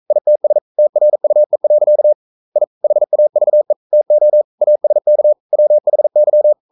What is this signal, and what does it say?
Morse code, heard as a keyed tone, about 42 wpm, 600 Hz. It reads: ITS TRUE. I HAVE TO ASK WHY
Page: Romanen Never af Ken Follet på engelsk.